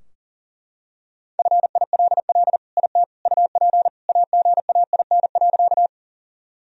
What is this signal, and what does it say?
Morse code, heard as a keyed tone, about 37 wpm, 700 Hz. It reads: FILL IT UP AGAIN.
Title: Sentences from Top 1000 Words - Review Part 3 - 40wpm